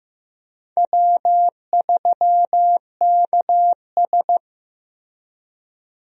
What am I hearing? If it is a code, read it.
Morse code, heard as a keyed tone, about 15 wpm, 700 Hz. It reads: W3KS
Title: USA Call Signs - 4 Characters - Repeated 3x - 15wpm